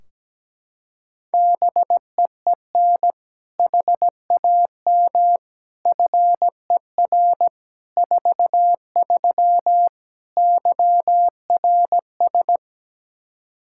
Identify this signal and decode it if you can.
Morse code, heard as a keyed tone, about 17 wpm, 700 Hz. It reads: BEEN HAM FER 43 YRS
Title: Practice QSO - Exchange 3 - More Info - 17wpm